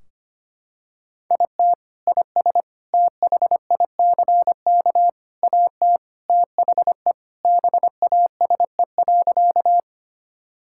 Morse code, heard as a keyed tone, about 25 wpm, 700 Hz. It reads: IT IS THICK AT THE BASE.
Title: Sentences from Top 1600 Words 25wpm